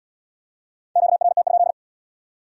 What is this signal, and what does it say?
Morse code, heard as a keyed tone, about 74 wpm, 700 Hz. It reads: QRT?